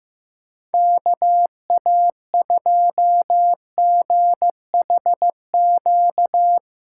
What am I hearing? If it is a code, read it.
Morse code, heard as a keyed tone, about 16 wpm, 700 Hz. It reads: KA2GHQ
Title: Speed Racing - USA 2x3 Call Signs 15wpm